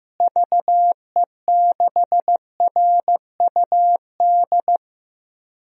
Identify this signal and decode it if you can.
Morse code, heard as a keyed tone, about 15 wpm, 700 Hz. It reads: VE6RUD
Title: DXCC-100 Call Signs - 6 Characters - Repeated 3x - 15wpm